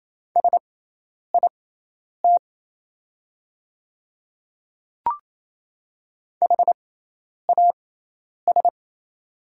Morse code, heard as a keyed tone, about 27 wpm, spaced as for 10 wpm, 690 Hz. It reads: SIT HAS
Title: 6x Character Spacing - Top 500 Words - 28wpm